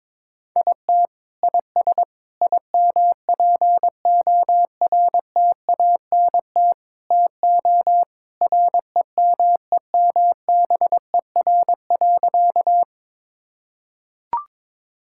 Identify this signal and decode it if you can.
Morse code, heard as a keyed tone, about 22 wpm, 700 Hz. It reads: IT IS IMPORTANT TO REMEMBER.
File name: practice / 22wpm / Sentences with 4-Word Phrases - 22wpm